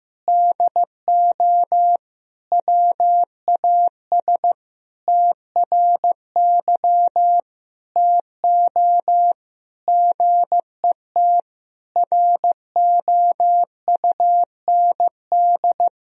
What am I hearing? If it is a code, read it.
Morse code, heard as a keyed tone, about 16 wpm, 700 Hz. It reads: DO WAS TRY TO GET ROUND
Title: Sentences from Top 1000 Words - Review Part 1 - 15wpm